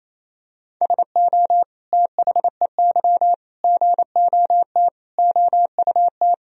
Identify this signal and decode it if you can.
Morse code, heard as a keyed tone, about 28 wpm, 700 Hz. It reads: SO THEY GOT OUT
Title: Sentences from Top 300 Words and Abbreviations 28wpm